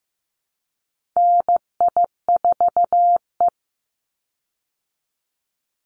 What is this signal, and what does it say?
Morse code, heard as a keyed tone, about 15 wpm, 700 Hz. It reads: NI4E
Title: Call Signs Repeated 2x - 15wpm